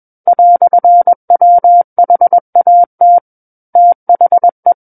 Morse code, heard as a keyed tone, about 21 wpm, 700 Hz. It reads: "WHAT THE
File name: Book One / 12-Chapter 12 - 21 WPM